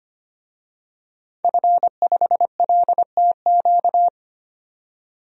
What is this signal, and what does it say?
Morse code, heard as a keyed tone, about 25 wpm, 700 Hz. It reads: F5LTQ
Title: DXCC-100 Call Signs - 5 Characters - 25wpm